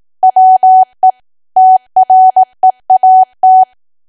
Morse code, heard as a keyed tone, about 18 wpm, 750 Hz. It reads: WE TREAT
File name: cwa / W1AW-Level2 / 18WPM